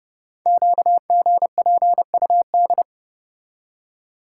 Morse code, heard as a keyed tone, about 30 wpm, 700 Hz. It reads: QGPUD
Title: Code Groups - 5 Letters - 30wpm